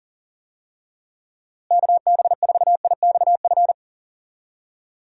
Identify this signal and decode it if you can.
Morse code, heard as a keyed tone, about 40 wpm, 700 Hz. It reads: KB4IXF
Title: USA Call Signs - 6 Characters - Repeated 3x - 40wpm